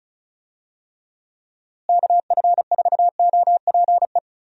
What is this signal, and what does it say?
Morse code, heard as a keyed tone, about 36 wpm, 700 Hz. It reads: KF4OPE
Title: USA Call Signs - 6 Characters - Repeated 2x - 35wpm